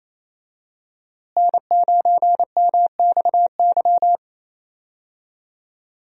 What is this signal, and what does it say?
Morse code, heard as a keyed tone, about 28 wpm, 700 Hz. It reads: N9MXY
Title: USA Call Signs - 5 Characters - Repeated 3x - 28wpm